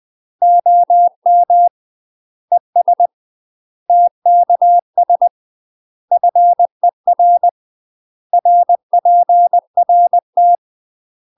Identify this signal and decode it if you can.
Morse code, heard as a keyed tone, about 20 wpm, 700 Hz. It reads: OM ES TKS FER RPRT
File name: UK MASTER QSOs TRAINING / 20WPM / UK MASTER QSOs 0045